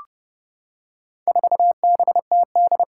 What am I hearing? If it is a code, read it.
Morse code, heard as a keyed tone, about 30 wpm, 700 Hz. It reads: 4BTD
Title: Four Challenging Letter-Number Combinations 30wpm